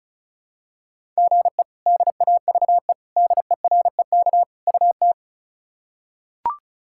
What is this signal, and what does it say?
Morse code, heard as a keyed tone, about 34 wpm, 700 Hz. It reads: GE DAVE DEREK UT
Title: K1USN SST QSO Practice - 35wpm